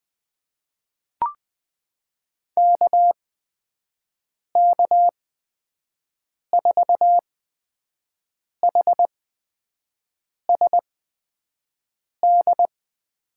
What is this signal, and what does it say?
Morse code, heard as a keyed tone, about 20 wpm, spaced as for 5 wpm, 700 Hz. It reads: KK4HSD
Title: USA Call Signs - 8x Character Spacing - 20wpm